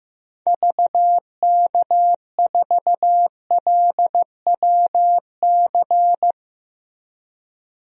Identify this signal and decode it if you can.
Morse code, heard as a keyed tone, about 15 wpm, 700 Hz. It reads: VK4LWC
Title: DXCC-100 Call Signs - 6 Characters - Repeated 3x - 15wpm